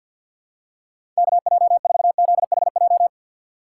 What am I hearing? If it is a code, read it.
Morse code, heard as a keyed tone, about 51 wpm, 700 Hz. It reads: KJ4ZHJ
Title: USA Call Signs - 6 Characters - 3x Character Spacing - 50wpm